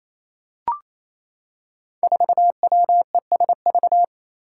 Morse code, heard as a keyed tone, about 27 wpm, 700 Hz. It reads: 4WESV
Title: Five Challenging Letter-Number Combinations 28wpm